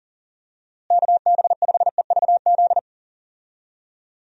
Morse code, heard as a keyed tone, about 37 wpm, 700 Hz. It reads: KB5EVZ
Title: USA Call Signs - 6 Characters - Repeated 2x - 40wpm